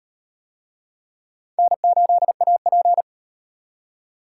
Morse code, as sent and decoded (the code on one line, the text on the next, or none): -. ---.. .- .--.
N8AP